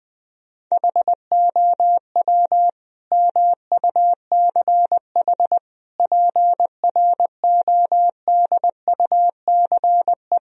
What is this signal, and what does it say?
Morse code, heard as a keyed tone, about 21 wpm, 700 Hz. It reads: HOW MUCH PRODUCE
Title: Sentences from Top 1000 Words - Review Part 3 - 20wpm